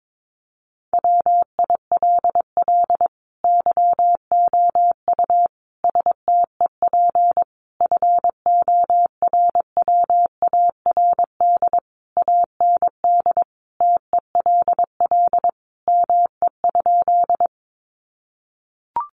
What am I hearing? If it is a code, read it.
Morse code, heard as a keyed tone, about 22 wpm, 700 Hz. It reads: WILL YOU STEP FORWARD AND TELL ME?
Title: Sentences from Top 1300 Words 22wpm